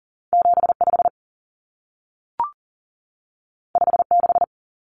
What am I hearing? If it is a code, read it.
Morse code, heard as a keyed tone, about 40 wpm, 700 Hz. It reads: 75 56
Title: Common Cut Numbers - 2-Digit 40wpm